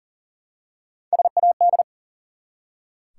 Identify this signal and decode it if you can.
Morse code, heard as a keyed tone, about 40 wpm, 700 Hz. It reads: SAD
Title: Three Letter Words - 40wpm